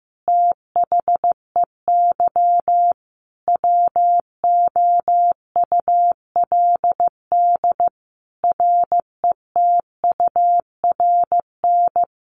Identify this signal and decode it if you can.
Morse code, heard as a keyed tone, about 14 wpm, 700 Hz. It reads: THEY WOULD RETURN
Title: Sentences from Top 1400 Words 15wpm